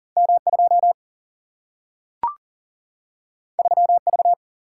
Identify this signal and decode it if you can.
Morse code, heard as a keyed tone, about 40 wpm, 700 Hz. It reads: M 2 3 V
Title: Single-Letter Single-Number 40wpm